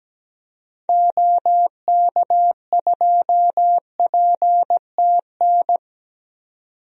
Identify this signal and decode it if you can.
Morse code, heard as a keyed tone, about 17 wpm, 700 Hz. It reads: OK2PTN
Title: DXCC-100 Call Signs - 6 Characters - 17wpm